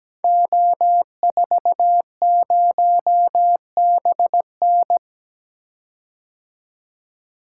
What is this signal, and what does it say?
Morse code, heard as a keyed tone, about 17 wpm, 700 Hz. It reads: O40BN
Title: Code Groups - 5 Letter:Numbers - 17wpm